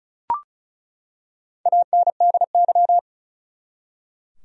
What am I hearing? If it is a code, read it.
Morse code, heard as a keyed tone, about 36 wpm, 700 Hz. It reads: ANDY
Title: USA Common Names - Max 4 Letters - 35wpm